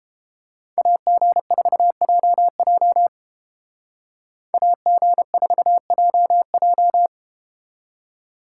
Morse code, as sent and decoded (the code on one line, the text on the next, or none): .- --. ....- .--- .--- / .- --. ....- .--- .---
AG4JJ AG4JJ